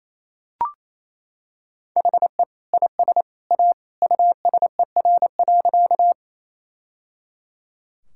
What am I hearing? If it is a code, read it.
Morse code, heard as a keyed tone, about 28 wpm, 700 Hz. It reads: HE IS A USER.
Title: Sentences from Top 1000 Words - Review Part 5 - 28wpm